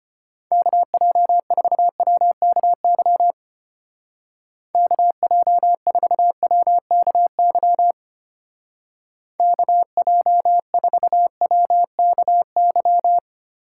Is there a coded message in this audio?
Yes: Morse code from 0.5 to 13.2 s, heard as a keyed tone, about 32 wpm, 700 Hz. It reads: KJ4WKY KJ4WKY KJ4WKY